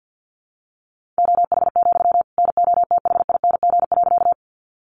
Morse code, heard as a keyed tone, about 50 wpm, 700 Hz. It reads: OH, NOTHING.
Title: Sentences from Top 500 Words 50wpm